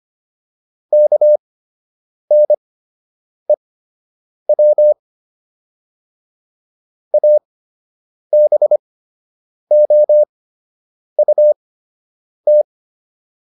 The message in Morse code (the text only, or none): KNEW ABOUT